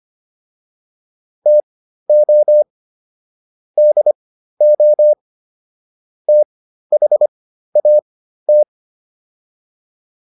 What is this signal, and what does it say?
Morse code, heard as a keyed tone, about 25 wpm, spaced as for 13 wpm, 600 Hz. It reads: TO DO THAT